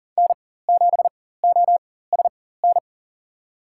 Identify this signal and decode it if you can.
Morse code, heard as a keyed tone, about 40 wpm, spaced as for 19 wpm, 700 Hz. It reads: N7OSN
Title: USA Call Signs - 4x Character Spacing - 40wpm